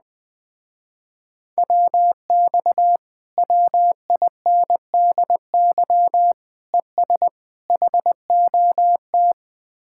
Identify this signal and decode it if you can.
Morse code, heard as a keyed tone, about 20 wpm, 700 Hz. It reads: WX WINDY ES HOT
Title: Speed-Racing Common QSO Phrases Encore - 17wpm